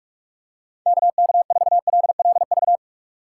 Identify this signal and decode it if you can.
Morse code, heard as a keyed tone, about 42 wpm, 700 Hz. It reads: KK4LLV
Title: USA Call Signs - 6 Characters - 3x Character Spacing - 45wpm